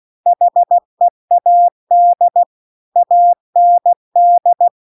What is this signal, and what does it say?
Morse code, heard as a keyed tone, about 16 wpm, 700 Hz. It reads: HEAD AND
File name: CW-The-Big-Twoheaded-River-2-16WPM